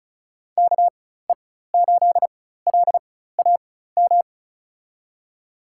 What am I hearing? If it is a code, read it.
Morse code, heard as a keyed tone, about 36 wpm, spaced as for 16 wpm, 700 Hz. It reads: KE8LAM
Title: USA Call Signs - 6 Characters - 4x Character Spacing - 35wpm